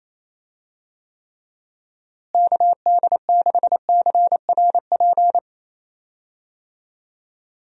Morse code, heard as a keyed tone, about 27 wpm, 700 Hz. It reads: KD6CRP
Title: USA Call Signs - 6 Characters - Repeated 3x - 28wpm